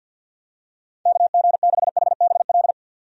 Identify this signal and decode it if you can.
Morse code, heard as a keyed tone, about 51 wpm, 700 Hz. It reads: KG6HBL
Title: USA Call Signs - 6 Characters - 50wpm